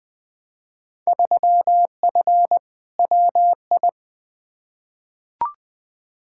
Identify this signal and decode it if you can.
Morse code, heard as a keyed tone, about 20 wpm, 700 Hz. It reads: 3F WI
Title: Field Day Exchanges - 20wpm